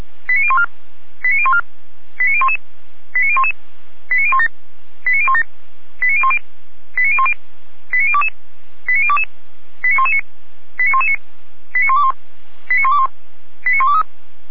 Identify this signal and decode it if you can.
Five-tone selective call (ZVEI), heard as quick runs of five tones, tones of about 70 ms. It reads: 89014 89014 89011 89011 89017 89017 89010 89010 89022 89022 89119 89119 89121 89121 89123